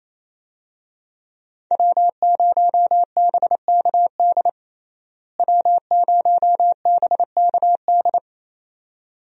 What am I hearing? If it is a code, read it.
Morse code, heard as a keyed tone, about 28 wpm, 700 Hz. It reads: W0BKD W0BKD
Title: USA Call Signs - 5 Characters - Repeated 2x - 28wpm